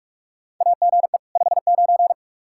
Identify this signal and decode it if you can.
Morse code, heard as a keyed tone, about 46 wpm, 700 Hz. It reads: AGE 59
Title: Practice QSO - Exchange 3 - More Info - 45wpm